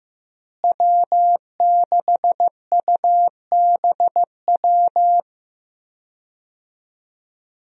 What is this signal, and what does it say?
Morse code, heard as a keyed tone, about 15 wpm, 700 Hz. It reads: W6UBW
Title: USA Call Signs - 5 Characters - Repeated 3x - 15wpm